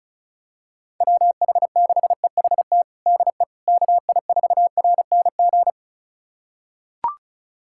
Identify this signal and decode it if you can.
Morse code, heard as a keyed tone, about 36 wpm, 700 Hz. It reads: WH6EHT DE KI4RNG
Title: USA Callsign DE Callsign 35wpm